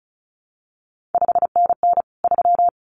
Morse code, heard as a keyed tone, about 36 wpm, 700 Hz. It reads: 5NN 3
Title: CQ WW DX Contest Exchange Encore - 35wpm